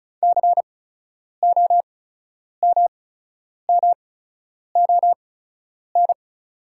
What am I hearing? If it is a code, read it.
Morse code, heard as a keyed tone, about 34 wpm, spaced as for 10 wpm, 690 Hz. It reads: COMMON
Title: 8x Character Spacing - Top 500 Words - 35wpm